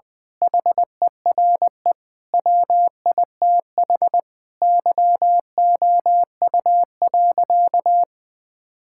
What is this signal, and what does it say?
Morse code, heard as a keyed tone, about 20 wpm, 700 Hz. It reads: HERE WITH YOU.
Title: Sentences from Top 1600 Words 20wpm